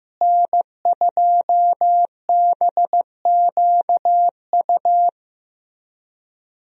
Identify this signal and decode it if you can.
Morse code, heard as a keyed tone, about 15 wpm, 700 Hz. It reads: N2BQU
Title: USA Call Signs - 5 Characters - Repeated 3x - 15wpm